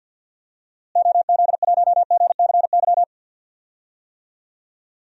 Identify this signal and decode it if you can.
Morse code, heard as a keyed tone, about 51 wpm, 700 Hz. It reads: OZ1GCY
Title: DXCC-100 Call Signs - 6 Characters - Repeated 3x - 50wpm